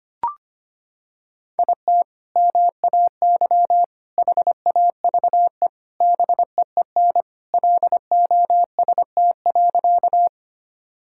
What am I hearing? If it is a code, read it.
Morse code, heard as a keyed tone, about 25 wpm, 700 Hz. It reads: IT MAY HAVE BEEN LOST.
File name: practice / 25wpm / Sentences with 3-Word Phrases - 25wpm